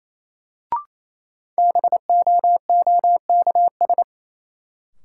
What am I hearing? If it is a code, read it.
Morse code, heard as a keyed tone, about 28 wpm, 700 Hz. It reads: BOOKS
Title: Sentences from Top 1000 Words - Review Part 4 - New Word Drill - 28wpm